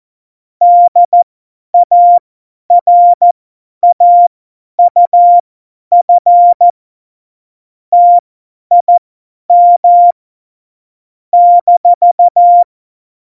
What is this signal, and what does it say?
Morse code, heard as a keyed tone, about 14 wpm, spaced as for 10 wpm, 700 Hz. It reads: DARAUF TIM -
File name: Morsekurs / Witze / Witz 26. November 2019, 50 F70